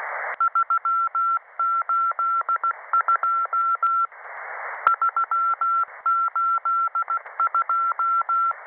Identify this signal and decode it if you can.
Morse code, heard as a keyed tone, about 16 wpm, 1350 Hz. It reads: 382 382
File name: websdr recording 6891.0kHz